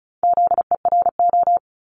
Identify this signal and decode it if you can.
Morse code, heard as a keyed tone, about 36 wpm, 700 Hz. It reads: ZERO
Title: Multi-letter Phonemes Repeated 3x 35wpm